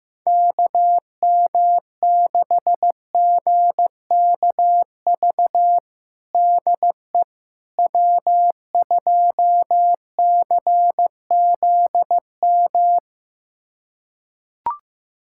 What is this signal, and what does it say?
Morse code, heard as a keyed tone, about 16 wpm, 700 Hz. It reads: KM6GKV DE W2CZM
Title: USA Callsign DE Callsign 15wpm